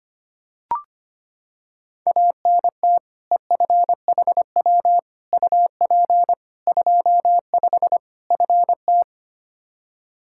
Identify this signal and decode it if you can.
Morse code, heard as a keyed tone, about 25 wpm, 700 Hz. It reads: ANT EFHW UP 25 FT
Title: K1USN SST QSO Practice 25wpm